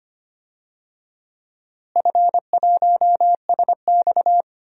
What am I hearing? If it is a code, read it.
Morse code, heard as a keyed tone, about 25 wpm, 700 Hz. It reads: F1SX
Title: DXCC-100 Call Signs - 4 Characters - Repeated 3x - 25wpm